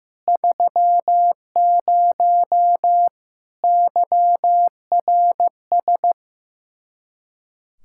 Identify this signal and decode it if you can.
Morse code, heard as a keyed tone, about 15 wpm, 700 Hz. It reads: 30 YRS
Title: Speed-Racing Common QSO Phrases Encore - 15wpm